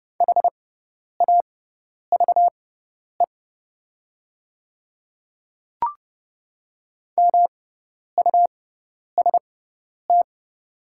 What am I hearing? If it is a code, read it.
Morse code, heard as a keyed tone, about 30 wpm, spaced as for 10 wpm, 700 Hz. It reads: HAVE MUST
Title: 6x Character Spacing - Top 500 Words - 30wpm